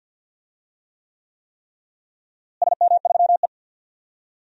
Morse code, heard as a keyed tone, about 50 wpm, 700 Hz. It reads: SM3E
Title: DXCC-100 Call Signs - 4 Characters - Repeated 3x - 50wpm